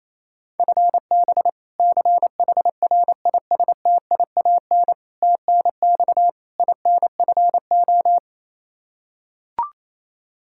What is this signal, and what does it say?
Morse code, heard as a keyed tone, about 27 wpm, 700 Hz. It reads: FB CHRISTIAN TNX INFO
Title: K1USN SST QSO Practice 28wpm